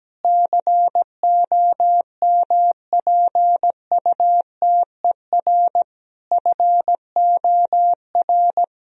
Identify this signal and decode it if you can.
Morse code, heard as a keyed tone, about 17 wpm, 700 Hz. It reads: COMPUTER FOR
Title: Sentences from Top 1000 Words - Review Part 3 - 17wpm